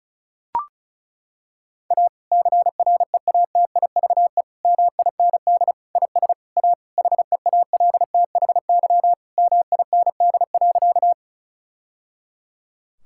Morse code, heard as a keyed tone, about 36 wpm, 690 Hz. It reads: A CREATIVE MIND IS A HEALTHY MIND.
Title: Sentences from Top 1000 Words - Review Part 5 - 35wpm